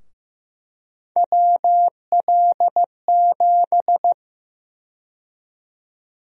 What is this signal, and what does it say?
Morse code, heard as a keyed tone, about 15 wpm, 700 Hz. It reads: WL7
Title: USA Call Sign Prefixes - Repeated 2x - 15wpm